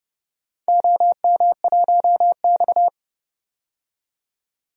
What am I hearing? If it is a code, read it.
Morse code, heard as a keyed tone, about 30 wpm, 700 Hz. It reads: OM1X
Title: DXCC-100 Call Signs - 4 Characters - 30wpm